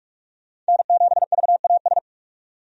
Morse code, heard as a keyed tone, about 46 wpm, 700 Hz. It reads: N7VAS
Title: USA Call Signs - 5 Characters - 45wpm